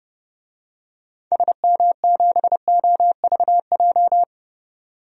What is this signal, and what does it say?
Morse code, heard as a keyed tone, about 30 wpm, 690 Hz. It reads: SM7OVJ
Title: DXCC-100 Call Signs - 6 Characters - 30wpm